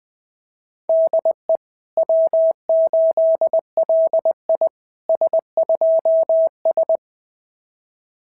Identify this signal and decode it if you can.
Morse code, heard as a keyed tone, about 21 wpm, 690 Hz. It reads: DE W8LI S2S
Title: SOTA QSO Practice 20wpm